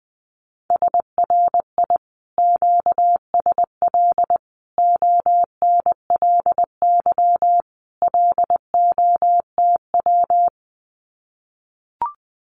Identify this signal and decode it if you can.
Morse code, heard as a keyed tone, about 20 wpm, 700 Hz. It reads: SRI QSL ONLY LOTW
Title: Common QSO Phrases Encore 20wpm